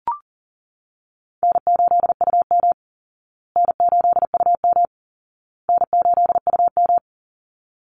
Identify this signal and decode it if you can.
Morse code, heard as a keyed tone, about 40 wpm, 700 Hz. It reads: N8UM N8UM N8UM
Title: Easy Call Signs Repeated 3x - 40wpm